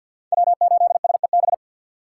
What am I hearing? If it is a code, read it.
Morse code, heard as a keyed tone, about 65 wpm, spaced as for 57 wpm, 700 Hz. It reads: W8SEB
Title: Speed Racing - USA 1x3 Call Signs 50wpm